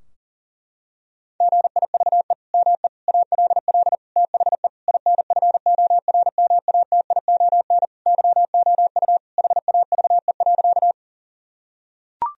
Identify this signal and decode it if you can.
Morse code, heard as a keyed tone, about 40 wpm, 700 Hz. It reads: GIVE ME ALL THE INFORMATION YOU HAVE.
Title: Sentences from Top 1000 Words - Review Part 3 - 40wpm